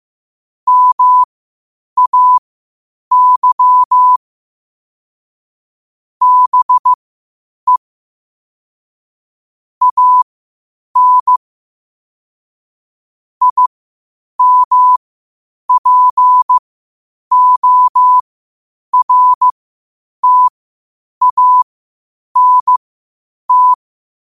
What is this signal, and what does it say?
Morse code, heard as a keyed tone, about 15 wpm, spaced as for 8 wpm, 1000 Hz. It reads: MAY BE AN IMPORTANT
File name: 8wpm